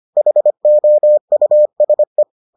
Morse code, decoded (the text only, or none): HOUSE